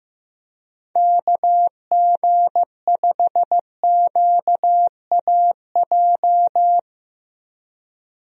Morse code, heard as a keyed tone, about 15 wpm, 700 Hz. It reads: KG5QAJ
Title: USA Call Signs - 6 Characters - Repeated 2x - 15wpm